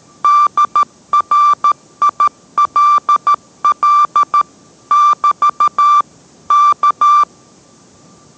Morse code, heard as a keyed tone, about 13 wpm, 1180 Hz. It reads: DRILL <BT> K